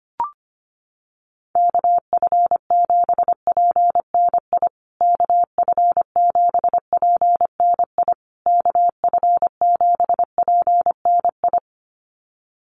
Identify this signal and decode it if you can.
Morse code, heard as a keyed tone, about 25 wpm, 700 Hz. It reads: KF7PNI KF7PNI KF7PNI
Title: USA Call Signs Repeated 3x 25wpm